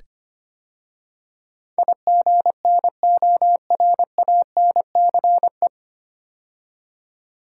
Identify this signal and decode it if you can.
Morse code, heard as a keyed tone, about 25 wpm, 700 Hz. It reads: IGNORANCE
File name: practice / 25wpm / 9 Letter Words - First 3 Letters Given - 25wpm